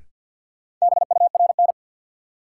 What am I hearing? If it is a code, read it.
Morse code, heard as a keyed tone, about 52 wpm, 700 Hz. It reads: BURN
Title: Sentences from Top 1300 Words - New Word Drill - 50wpm